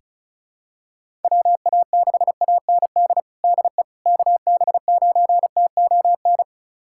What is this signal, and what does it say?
Morse code, heard as a keyed tone, about 34 wpm, 700 Hz. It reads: WA6AND DE KB9TON
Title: USA Callsign DE Callsign 35wpm